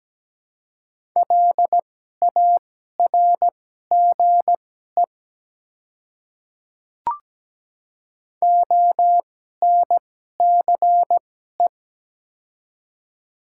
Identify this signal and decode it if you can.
Morse code, heard as a keyed tone, about 17 wpm, spaced as for 12 wpm, 690 Hz. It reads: LARGE ONCE
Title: 2x Character Spacing - Top 300 Words - 17wpm